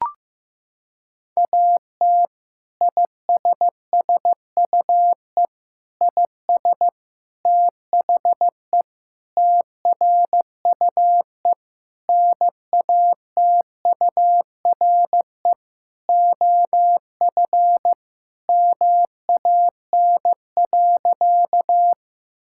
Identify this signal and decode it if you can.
Morse code, heard as a keyed tone, about 15 wpm, 700 Hz. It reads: AT ISSUE IS THE TRUE NATURE OF MAN.
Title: Sentences from Top 1000 Words - Review Part 6 - 15wpm